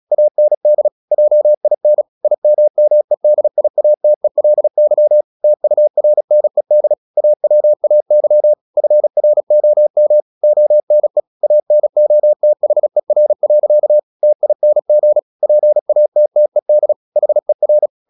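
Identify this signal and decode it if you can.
Morse code, heard as a keyed tone, about 36 wpm, 600 Hz. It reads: AND JIN IMMEDIATELY TURNED AWAY FROM ONE ANOTHER. TING PATTED HER
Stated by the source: Morse practice at 36 wpm, text of Ken Follett's novel Never, from CW med Gnister.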